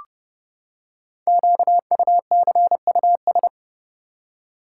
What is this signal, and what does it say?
Morse code, heard as a keyed tone, about 28 wpm, 700 Hz. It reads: QUCUS
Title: Code Groups - 5 Letters - 30wpm